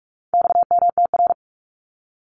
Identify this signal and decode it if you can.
Morse code, heard as a keyed tone, about 44 wpm, 700 Hz. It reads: XMTR
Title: Abbreviations 45wpm